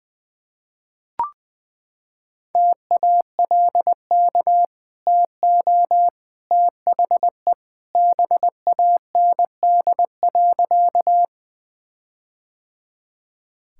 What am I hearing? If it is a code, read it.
Morse code, heard as a keyed tone, about 20 wpm, 700 Hz. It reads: TALK TO THE BAND.
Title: Sentences from Top 1600 Words 20wpm